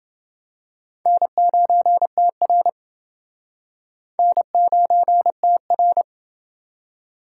Morse code, heard as a keyed tone, about 28 wpm, 700 Hz. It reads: N9TR N9TR